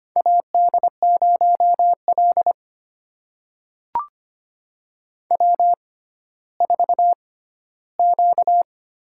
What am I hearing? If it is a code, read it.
Morse code, heard as a keyed tone, about 25 wpm, spaced as for 14 wpm, 700 Hz. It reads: AD0L W4Q
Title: USA Call Signs - 4 Characters - 6x Character Spacing - 25wpm